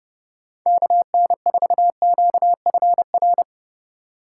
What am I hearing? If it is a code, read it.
Morse code, heard as a keyed tone, about 29 wpm, 700 Hz. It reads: KN4QFR